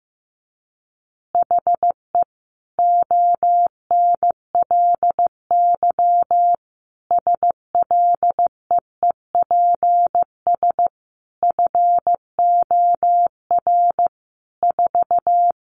Morse code, heard as a keyed tone, about 15 wpm, 700 Hz. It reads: HE ONLY SLEEPS FOR 4
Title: Additional Easy Phrases 15wpm